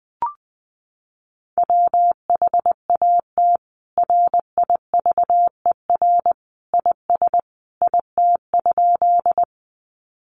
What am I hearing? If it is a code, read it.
Morse code, heard as a keyed tone, about 20 wpm, 700 Hz. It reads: WHAT RIVER IS IT?
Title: Sentences from Top 300 Words 20wpm